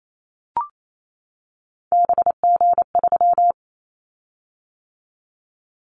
Morse code, heard as a keyed tone, about 28 wpm, 700 Hz. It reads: BG3
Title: Callsign Prefixes for Top 100 DXCC - 28wpm